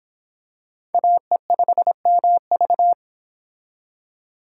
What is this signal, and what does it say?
Morse code, heard as a keyed tone, about 25 wpm, 700 Hz. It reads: AE5MV